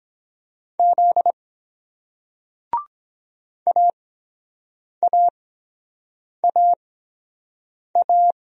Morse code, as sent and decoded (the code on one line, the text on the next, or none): --.. / .- .- .- .-
Z AAAA